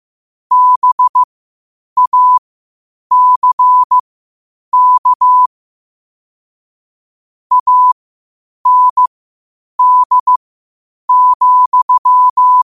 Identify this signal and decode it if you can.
Morse code, heard as a keyed tone, about 15 wpm, spaced as for 8 wpm, 1000 Hz. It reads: BACK AND,